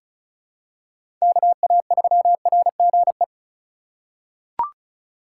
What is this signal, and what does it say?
Morse code, heard as a keyed tone, about 34 wpm, 700 Hz. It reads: KA3RGE
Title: USA Call Signs - 6 Characters - 2x Character Spacing - 35wpm